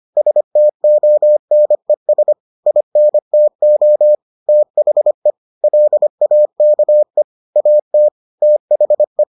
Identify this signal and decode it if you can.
Morse code, heard as a keyed tone, about 25 wpm, 600 Hz. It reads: STONES INTO THE LAKE AT THE